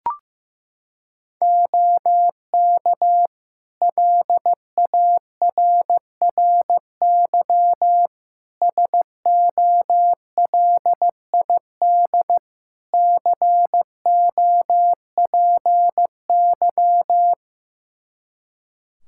Morse code, heard as a keyed tone, about 16 wpm, 700 Hz. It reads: OK LARRY SOLID COPY
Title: Practice QSO - Exchange 3 - More Info - 15wpm